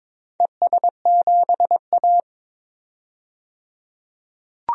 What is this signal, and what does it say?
Morse code, heard as a keyed tone, about 22 wpm, 700 Hz. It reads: ES7A
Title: DXCC-100 Call Signs - 4 Characters - 22wpm